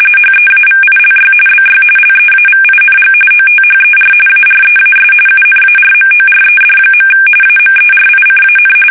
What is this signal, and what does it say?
RTTY, about 45.45 baud. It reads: HEAVILY BY LEAVING OTHER AREAS UNARMORED. AMERICAN
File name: fsk850 / battleshipnj